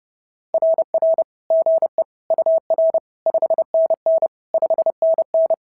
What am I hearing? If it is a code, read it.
Morse code, heard as a keyed tone, about 30 wpm, 650 Hz. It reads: RR GE UR 5NN 5NN
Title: SOTA QSO Practice 30wpm